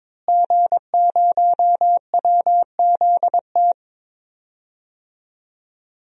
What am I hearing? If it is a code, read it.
Morse code, heard as a keyed tone, about 22 wpm, 700 Hz. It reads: G0WZT
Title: DXCC-100 Call Signs - 5 Characters - Repeated 2x - 22wpm